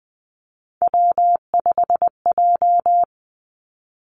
Morse code, heard as a keyed tone, about 20 wpm, 700 Hz. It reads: W5J
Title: Three Challenging Letter-Number Combinations 20wpm